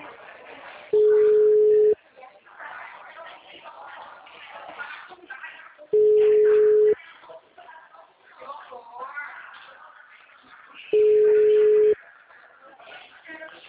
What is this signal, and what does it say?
Ringback tone (Europe, 425 Hz), with speech.